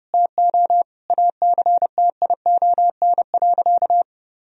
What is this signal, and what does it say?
Morse code, heard as a keyed tone, about 30 wpm, 700 Hz. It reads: TO ACTION.
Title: Sentences from Top 1000 Words - Review Part 4 - 30wpm